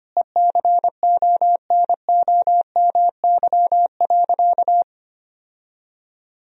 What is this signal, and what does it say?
Morse code, heard as a keyed tone, about 25 wpm, 700 Hz. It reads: ECONOMY.
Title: Sentences from Top 1000 Words - Review Part 4 - 25wpm